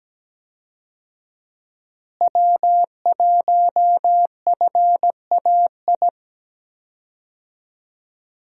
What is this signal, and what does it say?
Morse code, heard as a keyed tone, about 17 wpm, 700 Hz. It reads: W1FAI
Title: USA Call Signs - 5 Characters - Repeated 2x - 17wpm